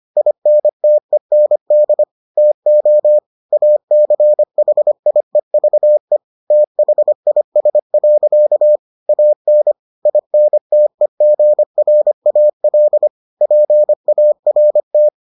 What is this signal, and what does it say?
Morse code, heard as a keyed tone, about 25 wpm, 600 Hz. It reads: INTEND TO ACHIEVE THIS. AN INTEGRAL PART